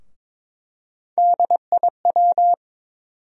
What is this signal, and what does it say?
Morse code, heard as a keyed tone, about 22 wpm, 700 Hz. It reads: DIW
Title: USA Call Sign Postfixes - 3 Letters 22wpm